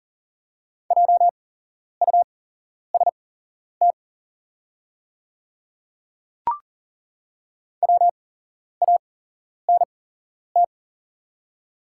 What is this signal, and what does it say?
Morse code, heard as a keyed tone, about 40 wpm, spaced as for 11 wpm, 690 Hz. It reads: JUST WANT
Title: 8x Character Spacing - Top 200 Words - 40wpm